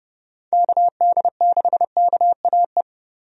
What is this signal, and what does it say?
Morse code, heard as a keyed tone, about 31 wpm, 700 Hz. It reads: KD6KAE
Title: USA Call Signs - 6 Characters - Repeated 2x - 30wpm